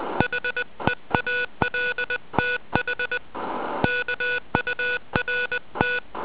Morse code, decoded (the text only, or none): HEALTH KURT